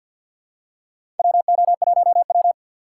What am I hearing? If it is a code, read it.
Morse code, heard as a keyed tone, about 52 wpm, 700 Hz. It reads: WO1W
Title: USA Call Signs - 3x Character Spacing - 50wpm